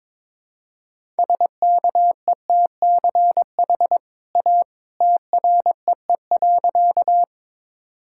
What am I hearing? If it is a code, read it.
Morse code, heard as a keyed tone, about 22 wpm, 700 Hz. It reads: SKETCH A TREE.
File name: practice / 22wpm / Easy Sentences with Hints - 22wpm